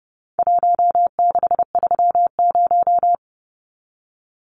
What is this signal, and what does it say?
Morse code, heard as a keyed tone, about 30 wpm, 700 Hz. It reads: 1630
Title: Numbers - Contest Style - 4-Digit 30wpm